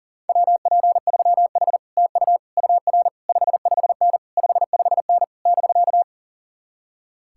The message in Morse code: WP3H TU UR 55N 55N <BK>

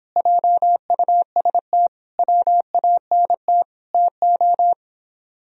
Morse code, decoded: JUST WANT TO

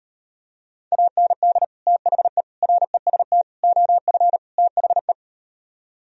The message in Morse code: AND THE REST OF THE

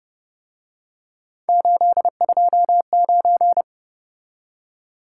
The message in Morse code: 829

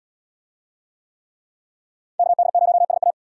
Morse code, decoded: KD0SA

